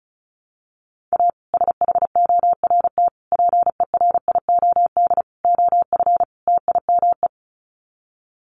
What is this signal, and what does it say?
Morse code, heard as a keyed tone, about 34 wpm, 700 Hz. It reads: A SHORT PERIOD OF TIME
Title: Sets of 5 Words Encore 35wpm